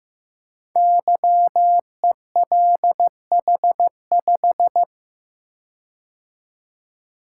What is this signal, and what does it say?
Morse code, heard as a keyed tone, about 15 wpm, 700 Hz. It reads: YELH5
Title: Code Groups - 5 Letter:Numbers - 15wpm